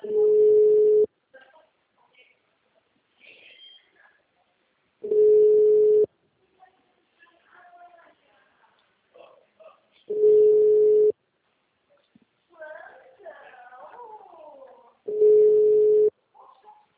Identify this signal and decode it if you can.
Ringback tone (Europe, 425 Hz), with speech.